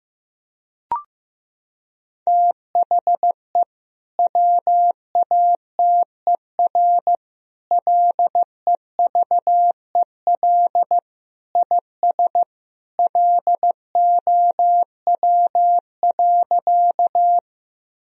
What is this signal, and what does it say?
Morse code, heard as a keyed tone, about 16 wpm, 700 Hz. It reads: THE WATER LEVEL IS LOW.
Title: Sentences from Top 1000 Words - Review Part 4 - 15wpm